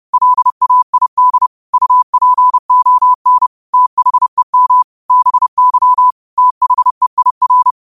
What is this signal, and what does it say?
Morse code, heard as a keyed tone, about 30 wpm, 1000 Hz. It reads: LAID UPON THEM BY THEIR
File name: qotdfiles / 30wpm